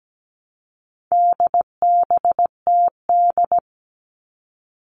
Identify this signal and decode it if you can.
Morse code, heard as a keyed tone, about 17 wpm, 700 Hz. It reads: DBTD
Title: All Cut Numbers - 4-Digit 17wpm